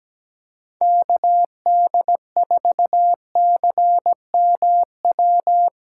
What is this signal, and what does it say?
Morse code, heard as a keyed tone, about 17 wpm, 700 Hz. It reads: KD4CMW
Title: USA Callsign DE Callsign 17wpm